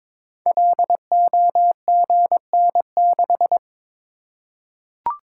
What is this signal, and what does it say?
Morse code, heard as a keyed tone, about 22 wpm, 700 Hz. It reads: LOGN6
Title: Code Groups - 5 Letter:Numbers - 22wpm